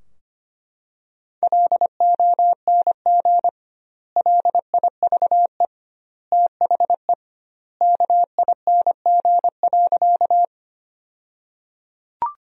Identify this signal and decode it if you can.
Morse code, heard as a keyed tone, about 25 wpm, 700 Hz. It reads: LONG LIVE THE KING.
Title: 2x Word-Spacing - Sentences from Top 400 Words - 25wpm